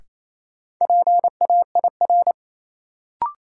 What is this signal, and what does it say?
Morse code, heard as a keyed tone, about 28 wpm, 700 Hz. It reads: PAIR
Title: Sentences from Top 1200 Words - New Word Drill Alt - 28wpm